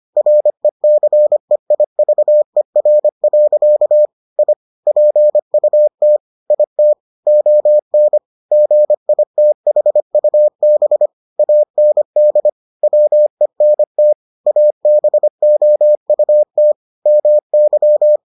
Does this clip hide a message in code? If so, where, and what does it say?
Yes: Morse code from 0.2 to 18.2 s, heard as a keyed tone, about 25 wpm, 600 Hz. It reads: RECEIVER. I PUT IT ON GITHUB AND WENT ABOUT MY